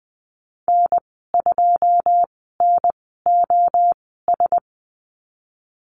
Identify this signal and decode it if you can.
Morse code, heard as a keyed tone, about 20 wpm, spaced as for 15 wpm, 700 Hz. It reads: N2NOS
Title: 088 Lesson 15g - Number 5 - Call Signs 20wpm